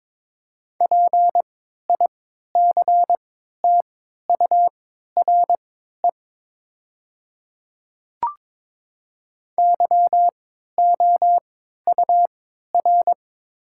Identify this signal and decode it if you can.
Morse code, heard as a keyed tone, about 22 wpm, spaced as for 13 wpm, 690 Hz. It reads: PICTURE YOUR
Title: 3x Character Spacing - Top 300 Words - 22wpm